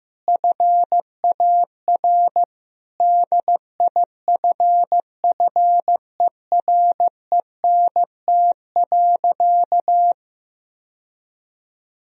Morse code, heard as a keyed tone, about 15 wpm, 700 Hz. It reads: FAR DIFFERENT.
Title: Sentences from Top 1000 Words - Review Part 3 - 15wpm